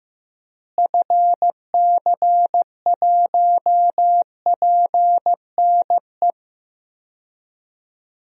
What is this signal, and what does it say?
Morse code, heard as a keyed tone, about 15 wpm, 700 Hz. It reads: FC1PNE
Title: DXCC-100 Call Signs - 6 Characters - Repeated 2x - 15wpm